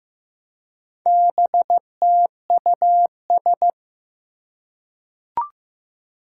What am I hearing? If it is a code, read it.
Morse code, heard as a keyed tone, about 15 wpm, 700 Hz. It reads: BTUS
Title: Four Challenging Letter-Number Combinations 15wpm